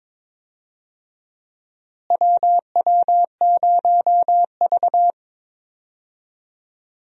Morse code, heard as a keyed tone, about 21 wpm, 700 Hz. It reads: WW0V
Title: USA Call Signs - 4 Characters - Repeated 3x - 22wpm